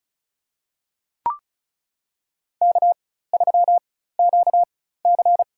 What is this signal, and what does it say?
Morse code, heard as a keyed tone, about 36 wpm, spaced as for 16 wpm, 700 Hz. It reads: K3QC
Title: USA Call Signs - 4 Characters - 4x Character Spacing - 35wpm